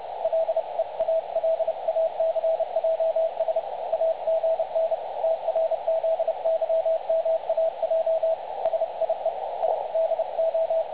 Morse code, heard as a keyed tone, about 29 wpm, 660 Hz. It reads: LEALACJS AGN TRZYMAJ SIE I DO
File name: dlazaawansowanych / nagraniazpasma / websdr recording 3565.0kHz